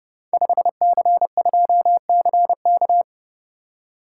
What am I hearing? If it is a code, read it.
Morse code, heard as a keyed tone, about 30 wpm, 700 Hz. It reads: 5C2CK
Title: Code Groups - 5 Letter:Numbers - 30wpm